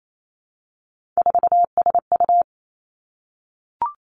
Morse code, heard as a keyed tone, about 28 wpm, 700 Hz. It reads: 4SU
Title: Three Challenging Letter-Number Combinations 28wpm